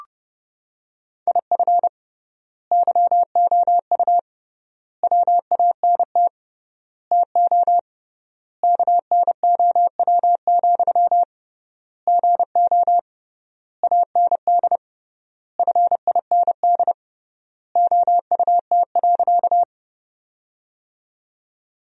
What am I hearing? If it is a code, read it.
Morse code, heard as a keyed tone, about 30 wpm, 700 Hz. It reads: IF YOU WANT TO KNOW, GO AND FIND OUT.
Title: 3x Word-Spacing - Sentences from Top 200 Words - 30wpm